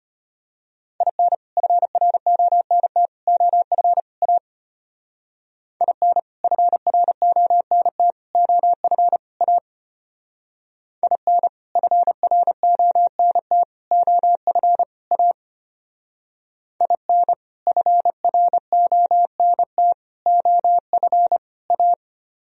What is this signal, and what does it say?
Morse code, heard as a keyed tone, about 32 wpm, 700 Hz. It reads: IN FRONT OF A IN FRONT OF A IN FRONT OF A IN FRONT OF A